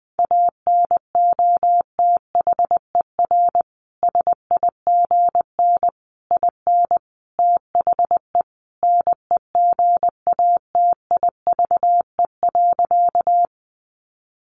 Morse code, heard as a keyed tone, about 20 wpm, 700 Hz. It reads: ANOTHER SIGN IN THE NEGATIVE.
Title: Sentences from Top 1000 Words 20wpm